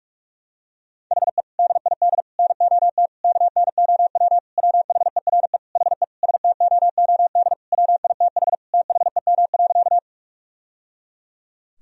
Morse code, heard as a keyed tone, about 46 wpm, 700 Hz. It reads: HE DID NOT KNOW WHERE HE STOOD WITH THEM.
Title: Sentences from Top 1000 Words - Review Part 2 - 45wpm